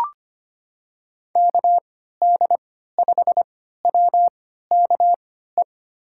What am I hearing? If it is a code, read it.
Morse code, heard as a keyed tone, about 25 wpm, spaced as for 14 wpm, 700 Hz. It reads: KD5WKE
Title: USA Call Signs - 6 Characters - 3x Character Spacing - 25wpm